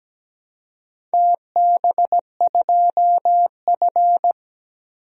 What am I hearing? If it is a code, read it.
Morse code, heard as a keyed tone, about 17 wpm, 700 Hz. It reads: TB2F
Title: DXCC-100 Call Signs - 4 Characters - 17wpm